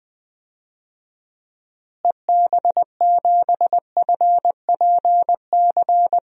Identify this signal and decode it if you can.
Morse code, heard as a keyed tone, about 20 wpm, 700 Hz. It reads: EB7FPC
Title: DXCC-100 Call Signs - 6 Characters - Repeated 3x - 20wpm